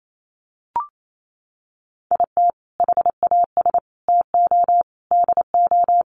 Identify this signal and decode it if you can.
Morse code, heard as a keyed tone, about 28 wpm, 700 Hz. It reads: IT HAS TO DO
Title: Sets of 5 Words 28wpm